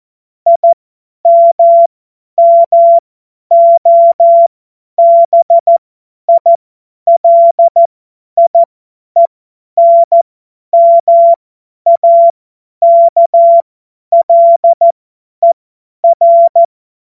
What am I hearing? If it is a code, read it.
Morse code, heard as a keyed tone, about 14 wpm, spaced as for 10 wpm, 680 Hz. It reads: IMMOBILIENMAKLER
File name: Morsekurs / Witze / Witz 27 Februar 2024 50 F70